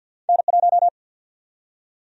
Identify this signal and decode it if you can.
Morse code, heard as a keyed tone, about 50 wpm, 700 Hz. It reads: N1
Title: USA Call Sign Prefixes - 50wpm